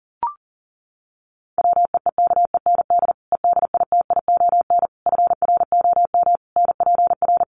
Morse code, heard as a keyed tone, about 40 wpm, 700 Hz. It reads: WEEKEND EDITION FROM NPR
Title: Sets of 5 Words 40wpm